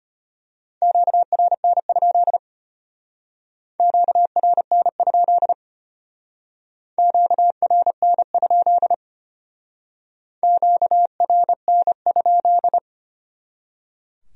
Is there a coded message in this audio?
Yes: Morse code from 0.8 to 12.8 s, heard as a keyed tone, about 32 wpm, 700 Hz. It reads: QRN? QRN? QRN? QRN?